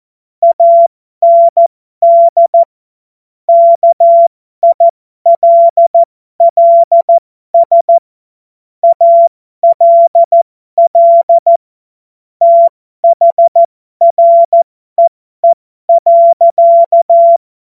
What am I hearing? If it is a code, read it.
Morse code, heard as a keyed tone, about 14 wpm, spaced as for 12 wpm, 680 Hz. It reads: AND KILLS ALL THREE.
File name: Morsekurs / Witze